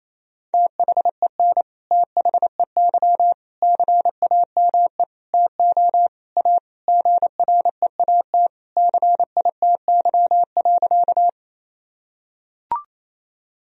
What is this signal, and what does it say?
Morse code, heard as a keyed tone, about 27 wpm, 700 Hz. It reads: THEN THEY CAME TO A GREAT CITY.
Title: Sentences from Top 1000 Words - Review Part 1 - 28wpm